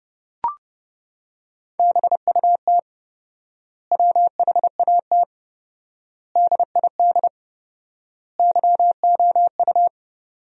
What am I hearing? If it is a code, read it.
Morse code, heard as a keyed tone, about 32 wpm, 700 Hz. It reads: BUT WHAT DID YOU
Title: 4x Word-Spacing - Sentences from Top 100 Words - 30wpm